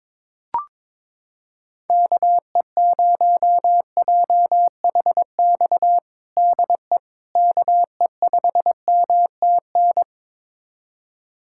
Morse code, heard as a keyed tone, about 22 wpm, 700 Hz. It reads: KE0JHX DE KE5MTN
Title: USA Callsign DE Callsign - 6 Characters - 22wpm